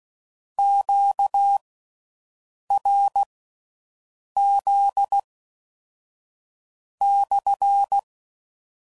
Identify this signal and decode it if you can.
Morse code, heard as a keyed tone, about 16 wpm, spaced as for 6 wpm, 800 Hz. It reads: QRZ /